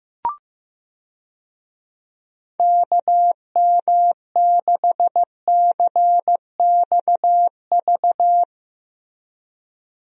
Morse code, heard as a keyed tone, about 15 wpm, 700 Hz. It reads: KM6CXV
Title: USA Call Signs - 6 Characters - Repeated 3x - 15wpm